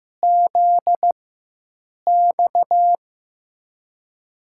Morse code, heard as a keyed tone, about 15 wpm, spaced as for 7 wpm, 700 Hz. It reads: ZX
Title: USA Call Signs - 4x Character Spacing - 15wpm